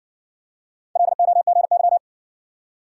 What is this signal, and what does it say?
Morse code, heard as a keyed tone, about 60 wpm, 700 Hz. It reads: LOGY